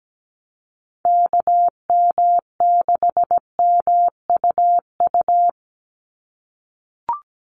Morse code, heard as a keyed tone, about 17 wpm, 700 Hz. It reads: KM6MUU
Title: USA Call Signs 17wpm